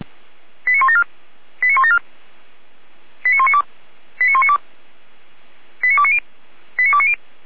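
Five-tone selective call (ZVEI), heard as quick runs of five tones, tones of about 70 ms. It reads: 89174 89174 89192 89192 89290 89290
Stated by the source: Five-tone ZVEI selective call received by SDR.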